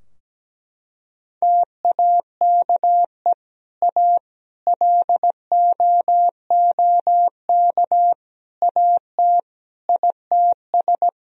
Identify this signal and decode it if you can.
Morse code, heard as a keyed tone, about 17 wpm, 700 Hz. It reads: TAKE A LOOK AT ITS
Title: Sentences from Top 1000 Words - Review Part 2 - 17wpm